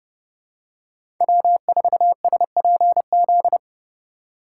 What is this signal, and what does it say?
Morse code, heard as a keyed tone, about 31 wpm, 700 Hz. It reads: W4SPZ
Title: USA Call Signs - 3x Character Spacing - 30wpm